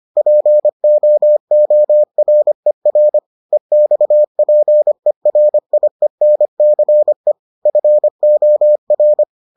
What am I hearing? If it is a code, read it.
Morse code, heard as a keyed tone, about 25 wpm, 600 Hz. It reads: POORER EXPERIENCE FOR